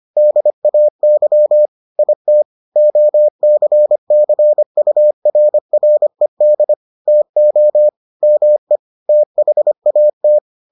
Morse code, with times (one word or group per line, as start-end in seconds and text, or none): 0.2-1.7 DAY
2.0-2.4 IT
2.8-6.7 OCCURRED
7.1-7.9 TO
8.2-8.8 ME
9.1-10.4 THAT